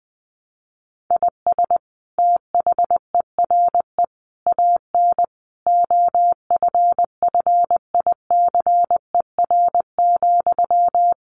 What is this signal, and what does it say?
Morse code, heard as a keyed tone, about 20 wpm, 700 Hz. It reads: IS THERE AN OFFICER,
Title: Sentences from Top 1200 Words 20wpm